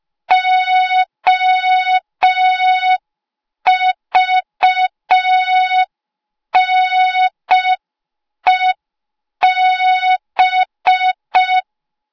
Morse code, heard as a keyed tone, about 5 wpm, 3700 Hz. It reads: OVNEB